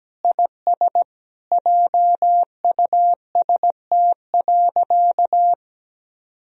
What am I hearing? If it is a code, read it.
Morse code, heard as a keyed tone, about 17 wpm, 700 Hz. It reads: IS JUST.
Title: Sentences from Top 1000 Words - Review Part 1 - 17wpm